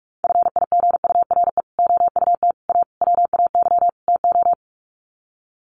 Morse code, heard as a keyed tone, about 44 wpm, 700 Hz. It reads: FIGURE OUT A WAY TO
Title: Sets of 5 Words Encore2 45wpm